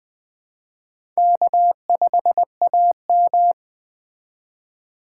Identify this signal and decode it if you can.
Morse code, heard as a keyed tone, about 20 wpm, 700 Hz. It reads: K5AM
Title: K1USN SST QSO Practice - 20wpm